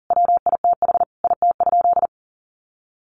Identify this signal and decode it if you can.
Morse code, heard as a keyed tone, about 37 wpm, 700 Hz. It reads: WITH IT?
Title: Sentences from Top 100 Words 40wpm